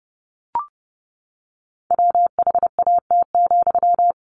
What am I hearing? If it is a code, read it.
Morse code, heard as a keyed tone, about 28 wpm, 700 Hz. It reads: WHAT,
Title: Sentences from Top 1100 Words 30wpm